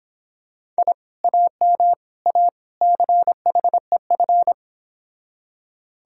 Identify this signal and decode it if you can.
Morse code, heard as a keyed tone, about 26 wpm, 700 Hz. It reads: I AM A CHEF